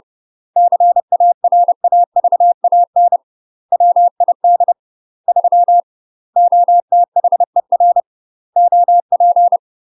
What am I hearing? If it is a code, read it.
Morse code, heard as a keyed tone, about 30 wpm, 700 Hz. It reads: CARAVAN WID 3 OTHER OP
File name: UK MASTER QSOs TRAINING / 30WPM / UK MASTER QSOs 0008